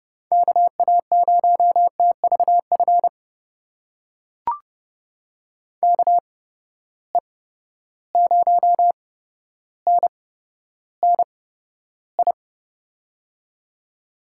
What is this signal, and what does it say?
Morse code, heard as a keyed tone, about 30 wpm, spaced as for 13 wpm, 700 Hz. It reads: KA0TVF KE0NNI
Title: USA Call Signs - 6 Characters - 8x Character Spacing - 30wpm